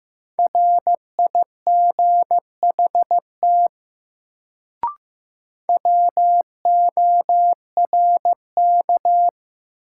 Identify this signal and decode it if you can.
Morse code, heard as a keyed tone, about 15 wpm, 700 Hz. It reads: RIGHT WORK
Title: Sets of 2 Words - New Word Drill - 15wpm